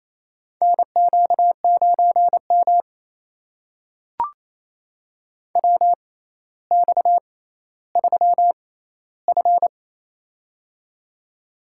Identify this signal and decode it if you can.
Morse code, heard as a keyed tone, about 28 wpm, spaced as for 14 wpm, 700 Hz. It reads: NQ9M WX3F
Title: USA Call Signs - 4 Characters - 6x Character Spacing - 28wpm